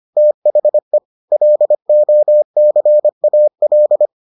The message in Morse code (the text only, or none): THE LOCAL